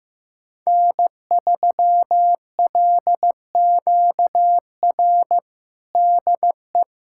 Morse code, heard as a keyed tone, about 15 wpm, 700 Hz. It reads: N3LQR DE
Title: USA Callsign DE Callsign - 5 Characters - 15wpm